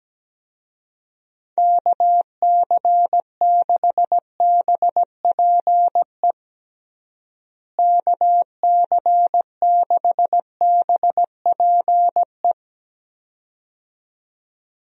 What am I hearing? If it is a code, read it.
Morse code, heard as a keyed tone, about 17 wpm, 700 Hz. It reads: KC6BPE KC6BPE
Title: USA Call Signs - 6 Characters - Repeated 2x - 17wpm